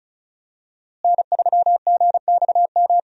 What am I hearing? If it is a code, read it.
Morse code, heard as a keyed tone, about 36 wpm, 700 Hz. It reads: N3GXM
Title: USA Call Signs - 5 Characters - Repeated 2x - 35wpm